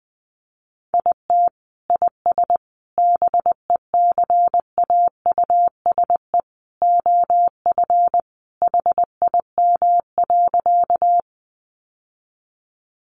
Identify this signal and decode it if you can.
Morse code, heard as a keyed tone, about 20 wpm, 700 Hz. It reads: IT IS BECAUSE OF HIM.
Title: Sentences from Top 600 Words 20wpm